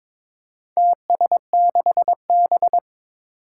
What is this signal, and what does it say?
Morse code, heard as a keyed tone, about 22 wpm, 700 Hz. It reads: TS6B
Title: Four Challenging Letter-Number Combinations 22wpm